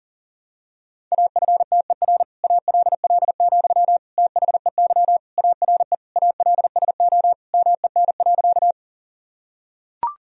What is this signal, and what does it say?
Morse code, heard as a keyed tone, about 40 wpm, 700 Hz. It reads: AFTER ALL, THEY ARE ALSO MEN.
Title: Sentences from Top 1000 Words - Review Part 1 - 40wpm